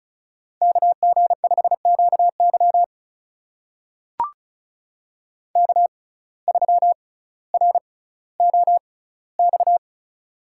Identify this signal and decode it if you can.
Morse code, heard as a keyed tone, about 36 wpm, spaced as for 18 wpm, 700 Hz. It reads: KG5QY K3ROX
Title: USA Call Signs - 5 Characters - 6x Character Spacing - 35wpm